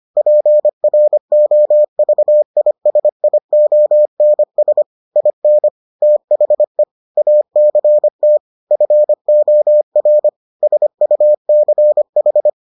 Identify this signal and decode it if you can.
Morse code, heard as a keyed tone, about 25 wpm, 600 Hz. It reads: PROVISIONS IN THE ACT FOR SUCH